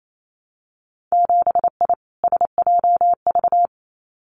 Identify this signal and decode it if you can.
Morse code, heard as a keyed tone, about 28 wpm, 700 Hz. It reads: 7I SJV
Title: Winter Field Day Exchanges Encore 28wpm